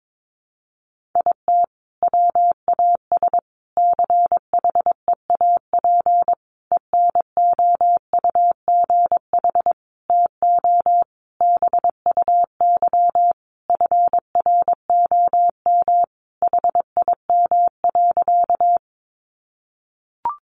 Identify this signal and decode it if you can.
Morse code, heard as a keyed tone, about 22 wpm, 700 Hz. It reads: IT WAS CHEAP ENOUGH TO BUY FROM HIM.
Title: Sentences from Top 1200 Words 22wpm